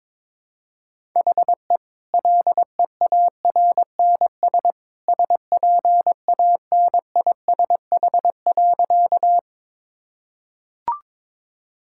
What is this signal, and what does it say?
Morse code, heard as a keyed tone, about 22 wpm, 700 Hz. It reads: HE LEARNS SPANISH.
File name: practice / 22wpm / Easy Sentences with Hints - 22wpm